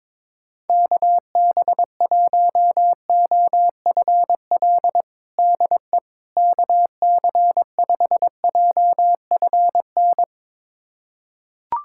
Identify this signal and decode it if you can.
Morse code, heard as a keyed tone, about 22 wpm, 700 Hz. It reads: KB1OFL DE KC5JFN
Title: USA Callsign DE Callsign - 6 Characters - 22wpm